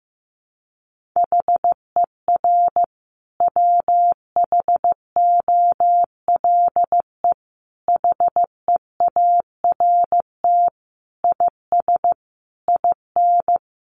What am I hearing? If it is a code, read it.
Morse code, heard as a keyed tone, about 15 wpm, 700 Hz. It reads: HER WHOLE HEART IS IN
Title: Sentences from Top 1500 Words 15wpm